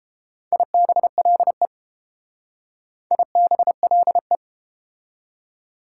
Morse code, heard as a keyed tone, about 29 wpm, 700 Hz. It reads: IBLE IBLE